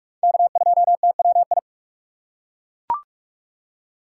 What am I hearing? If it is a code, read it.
Morse code, heard as a keyed tone, about 41 wpm, 700 Hz. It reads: K2TWI